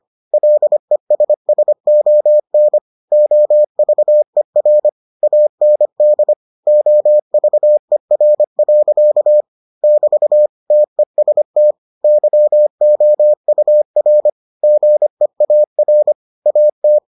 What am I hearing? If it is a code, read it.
Morse code, heard as a keyed tone, about 25 wpm, 600 Hz. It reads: LESSON OVER AND OVER. <BT> TEST YOUR GEAR AT